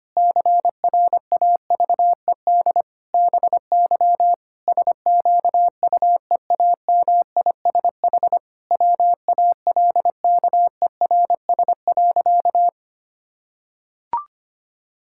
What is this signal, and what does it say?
Morse code, heard as a keyed tone, about 25 wpm, 700 Hz. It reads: CRAVED BY SQUEAMISH WALKERS.
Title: Pangram Challenge v2 - 25wpm